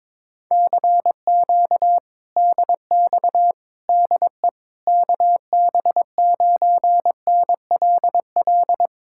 Morse code, heard as a keyed tone, about 21 wpm, 700 Hz. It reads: CQ DX DE KB9NLL
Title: Speed-Racing Common QSO Phrases - 22wpm